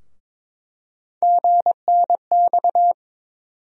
Morse code, heard as a keyed tone, about 22 wpm, 700 Hz. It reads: GNX
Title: USA Call Sign Postfixes - 3 Letters 22wpm